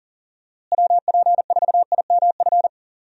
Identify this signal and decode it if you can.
Morse code, heard as a keyed tone, about 40 wpm, 700 Hz. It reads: WP4IMF
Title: USA Call Signs - 6 Characters - Repeated 2x - 40wpm